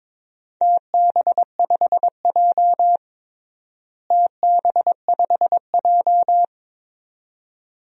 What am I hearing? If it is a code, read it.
Morse code, heard as a keyed tone, about 22 wpm, 700 Hz. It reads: TB5J TB5J
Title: DXCC-100 Call Signs - 4 Characters - Repeated 2x - 22wpm